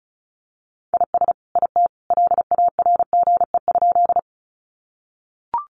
Morse code, heard as a keyed tone, about 34 wpm, 700 Hz. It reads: IS IT LARGE?
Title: Sentences from Top 200 Words 35wpm